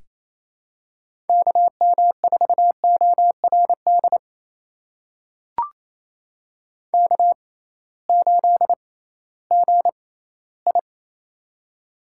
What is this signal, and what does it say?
Morse code, heard as a keyed tone, about 28 wpm, spaced as for 16 wpm, 700 Hz. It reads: KM4ORD K8GI
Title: USA Call Signs - 6x Character Spacing - 28wpm